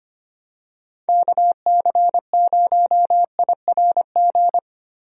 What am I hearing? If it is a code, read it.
Morse code, heard as a keyed tone, about 25 wpm, 700 Hz. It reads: KC0IRG
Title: USA Call Signs - 6 Characters - Repeated 2x - 25wpm